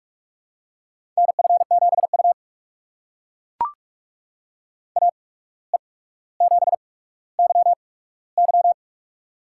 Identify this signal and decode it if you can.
Morse code, heard as a keyed tone, about 42 wpm, spaced as for 17 wpm, 700 Hz. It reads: NF7U AE7YY
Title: USA Call Signs - 8x Character Spacing - 45wpm